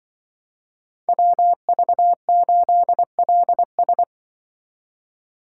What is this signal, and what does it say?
Morse code, heard as a keyed tone, about 24 wpm, 700 Hz. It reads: WV8LS
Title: Speed Racing - USA Call Signs 20wpm